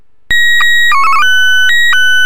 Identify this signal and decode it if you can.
SSTV (slow-scan television), Wraase SC2-180.